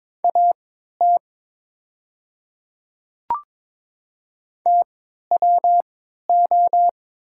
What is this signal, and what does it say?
Morse code, heard as a keyed tone, about 23 wpm, spaced as for 13 wpm, 700 Hz. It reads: AT TWO
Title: 3x Character Spacing - Top 200 Words - 22wpm